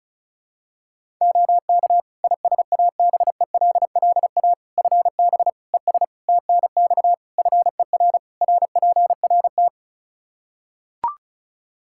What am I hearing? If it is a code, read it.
Morse code, heard as a keyed tone, about 36 wpm, 700 Hz. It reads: OK ISABELLA FB ES TNX FER RPRT
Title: Practice QSO - Exchange 2 - Info - 35wpm